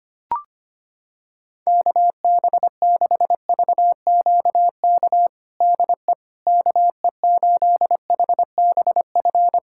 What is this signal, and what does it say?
Morse code, heard as a keyed tone, about 25 wpm, 700 Hz. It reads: KB6VQK DE KE8HBF
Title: USA Callsign DE Callsign - 6 Characters - 25wpm